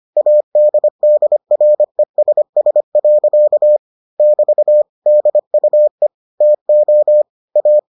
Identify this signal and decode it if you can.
Morse code, heard as a keyed tone, about 25 wpm, 600 Hz. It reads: ADDRESS. <BT> DUE TO A